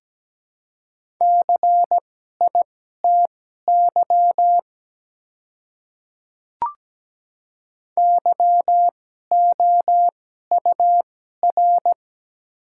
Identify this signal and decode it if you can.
Morse code, heard as a keyed tone, about 17 wpm, spaced as for 12 wpm, 700 Hz. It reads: CITY YOUR
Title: 2x Character Spacing - Top 300 Words - 17wpm